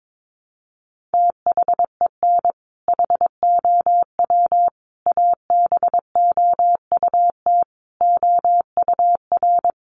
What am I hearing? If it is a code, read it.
Morse code, heard as a keyed tone, about 22 wpm, 700 Hz. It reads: THEN HOW ABOUT OUR
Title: Sentences from Top 900 Words 22wpm